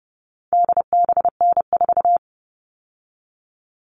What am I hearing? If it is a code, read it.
Morse code, heard as a keyed tone, about 30 wpm, 700 Hz. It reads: DBN4
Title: All Cut Numbers - 4-Digit 30wpm